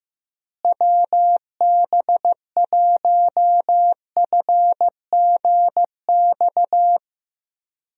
Morse code, heard as a keyed tone, about 15 wpm, 700 Hz. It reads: WB1FGX
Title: USA Call Signs - 6 Characters - Repeated 3x - 15wpm